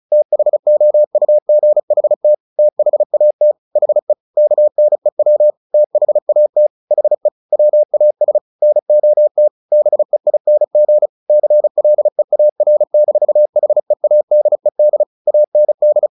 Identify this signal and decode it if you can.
Morse code, heard as a keyed tone, about 34 wpm, 600 Hz. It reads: THOUGHT THAT HE KNEW THAT HE WAS NOT BEING CLEAR-HEADED AND
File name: the old man and the sea 2 35wpm